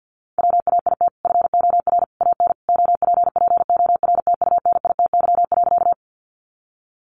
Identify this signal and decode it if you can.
Morse code, heard as a keyed tone, about 50 wpm, 700 Hz. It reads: WAIT FOR AN OPPORTUNITY.
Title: Sentences from Top 1000 Words 50wpm